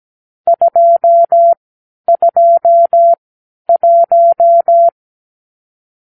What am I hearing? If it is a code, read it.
Morse code, heard as a keyed tone, about 17 wpm, spaced as for 10 wpm, 680 Hz. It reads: 221
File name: k7qo / Lesson 44 (1-3)